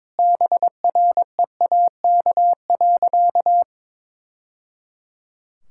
Morse code, heard as a keyed tone, about 22 wpm, 700 Hz. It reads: BREAK.
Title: Sentences from Top 1000 Words - Review Part 6 - 22wpm